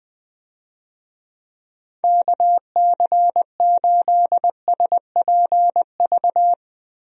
Morse code, heard as a keyed tone, about 21 wpm, 700 Hz. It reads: KC8SPV
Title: USA Call Signs - 6 Characters - 20wpm